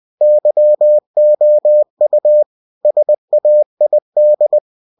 Morse code, heard as a keyed tone, about 20 wpm, 600 Hz. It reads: YOU SAID
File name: cw-books / winnie-the-pooh / winnie-20wpm-page-0077